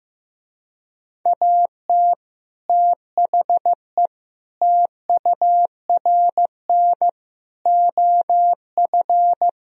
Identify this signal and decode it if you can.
Morse code, heard as a keyed tone, about 15 wpm, 700 Hz. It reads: AT THE TURN OF
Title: Speed-Racing - Sets of 5 Words 15wpm